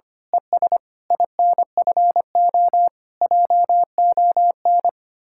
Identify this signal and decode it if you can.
Morse code, heard as a keyed tone, about 24 wpm, 700 Hz. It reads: ES INFO JON
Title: Speed-Racing Common QSO Phrases - 25wpm